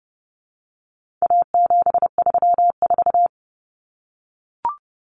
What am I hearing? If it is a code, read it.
Morse code, heard as a keyed tone, about 30 wpm, 700 Hz. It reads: A734
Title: Common Cut Numbers - 4-Digit 30wpm